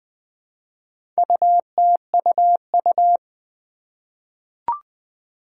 Morse code, heard as a keyed tone, about 20 wpm, 700 Hz. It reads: UTUU
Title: Four Challenging Letter-Number Combinations 20wpm